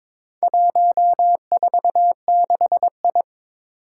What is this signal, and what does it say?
Morse code, heard as a keyed tone, about 22 wpm, 700 Hz. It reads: 146I
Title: Four Challenging Letter-Number Combinations 22wpm